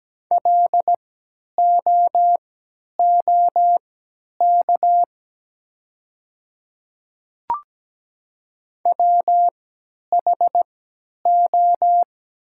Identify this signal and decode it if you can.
Morse code, heard as a keyed tone, about 17 wpm, spaced as for 10 wpm, 700 Hz. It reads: LOOK WHO
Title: 3x Character Spacing - Top 100 Words - 17wpm